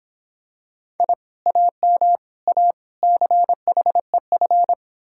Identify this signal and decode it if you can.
Morse code, heard as a keyed tone, about 25 wpm, 700 Hz. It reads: I AM A CHEF